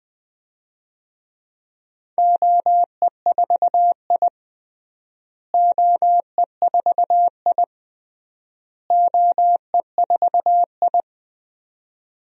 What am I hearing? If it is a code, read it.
Morse code, heard as a keyed tone, about 20 wpm, 700 Hz. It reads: OE4I OE4I OE4I
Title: DXCC-100 Call Signs - 4 Characters - Repeated 3x - 20wpm